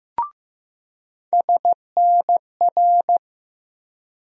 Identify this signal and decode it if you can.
Morse code, heard as a keyed tone, about 15 wpm, 700 Hz. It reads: SNR
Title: USA Call Sign Postfixes - 3 Letters 15wpm